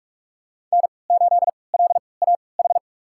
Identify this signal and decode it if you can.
Morse code, heard as a keyed tone, about 46 wpm, spaced as for 26 wpm, 700 Hz. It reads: N8LAH
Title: USA Call Signs - 5 Characters - 3x Character Spacing - 45wpm